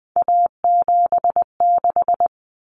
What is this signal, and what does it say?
Morse code, heard as a keyed tone, about 20 wpm, 700 Hz. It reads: A76
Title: Common Cut Numbers - 3-Digit 20wpm